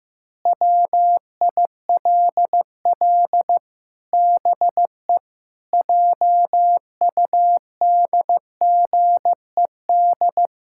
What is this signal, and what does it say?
Morse code, heard as a keyed tone, about 15 wpm, 700 Hz. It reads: WILL BE JUDGED
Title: Pangram Challenge v2 - 15wpm